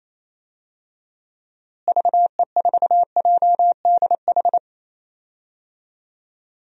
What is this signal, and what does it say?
Morse code, heard as a keyed tone, about 28 wpm, 700 Hz. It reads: VE4JDH
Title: DXCC-100 Call Signs - 6 Characters - Repeated 2x - 28wpm